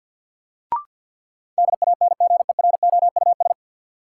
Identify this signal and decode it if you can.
Morse code, heard as a keyed tone, about 53 wpm, 700 Hz. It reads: DANGEROUS
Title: Sentences from Top 1200 Words - New Word Drill - 50wpm